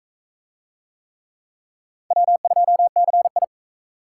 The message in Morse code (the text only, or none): W2CI